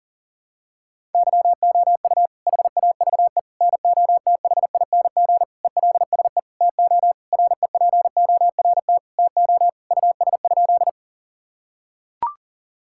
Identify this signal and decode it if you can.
Morse code, heard as a keyed tone, about 40 wpm, 700 Hz. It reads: YOU HAVE NOTHING ELSE TO REPORT TO US?
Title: Sentences from Top 1000 Words - Review Part 6 - 40wpm